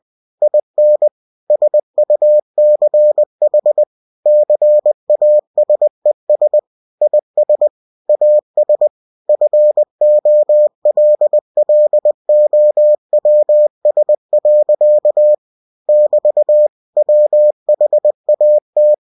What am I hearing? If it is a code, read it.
Morse code, heard as a keyed tone, about 20 wpm, 600 Hz. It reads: IN SUCH CASES IS AS FOLLOWS. <BT> WHAT
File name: cw-books / winnie-the-pooh / winnie-20wpm-page-0029